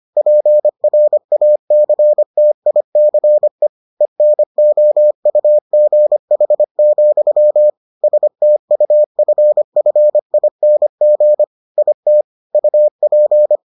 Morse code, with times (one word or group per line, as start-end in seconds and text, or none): 0.2-3.7 PRACTICE
4.0-7.7 ENOUGH,
8.0-11.4 STUFFING
11.8-12.2 IT
12.5-13.6 UP